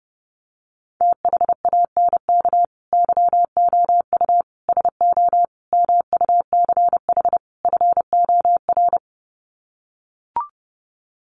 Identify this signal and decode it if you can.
Morse code, heard as a keyed tone, about 30 wpm, 700 Hz. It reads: THANK YOU SO MUCH FOR
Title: Sets of 5 Words 30wpm